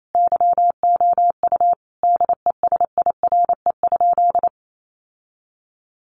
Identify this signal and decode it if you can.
Morse code, heard as a keyed tone, about 28 wpm, 700 Hz. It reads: YOU DESIRE?
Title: Sentences from Top 1100 Words 28wpm